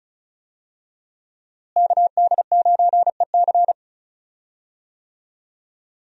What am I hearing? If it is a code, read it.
Morse code, heard as a keyed tone, about 34 wpm, 700 Hz. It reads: KD9EC
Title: USA Call Signs - 5 Characters - Repeated 2x - 35wpm